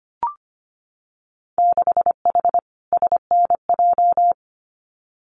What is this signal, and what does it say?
Morse code, heard as a keyed tone, about 25 wpm, 700 Hz. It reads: 6H SNJ
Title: Winter Field Day Exchanges Encore 25wpm